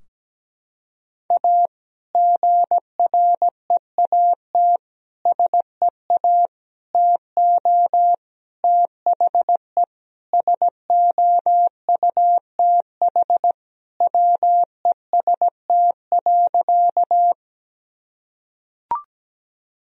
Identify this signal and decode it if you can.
Morse code, heard as a keyed tone, about 17 wpm, 700 Hz. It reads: A GREAT SEA TO THE SOUTH WEST.
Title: Sentences from Top 1000 Words - Review Part 2 - 17wpm